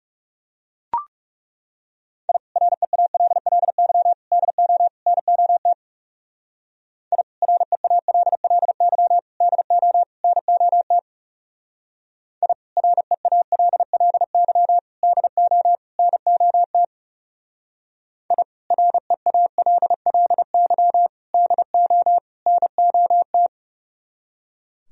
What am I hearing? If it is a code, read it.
Morse code, heard as a keyed tone, about 36 wpm, 700 Hz. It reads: I REALLY DO NOT I REALLY DO NOT I REALLY DO NOT I REALLY DO NOT
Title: Speed-Racing - Sets of 4 Words 30wpm